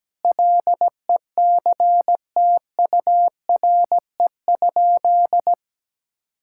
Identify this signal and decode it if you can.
Morse code, heard as a keyed tone, about 17 wpm, 700 Hz. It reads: LECTURE?
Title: Sentences from Top 1600 Words 17wpm